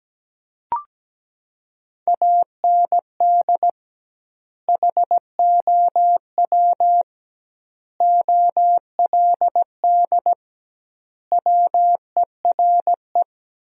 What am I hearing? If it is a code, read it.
Morse code, heard as a keyed tone, about 17 wpm, 700 Hz. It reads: AND HOW OLD WERE
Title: 2x Word-Spacing - Sentences from Top 200 Words - 17wpm